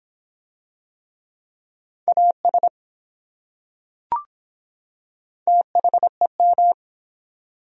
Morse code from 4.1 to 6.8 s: THEM

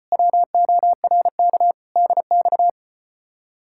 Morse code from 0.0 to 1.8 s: WORK

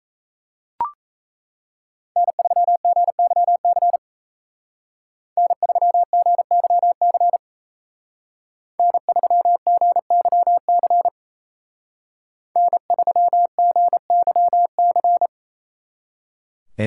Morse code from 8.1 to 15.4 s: N3GYC N3GYC

